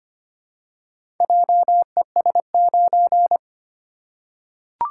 Morse code, heard as a keyed tone, about 25 wpm, 700 Hz. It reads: JES9
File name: practice / 25wpm / Four Letter-Number Combinations - 25wpm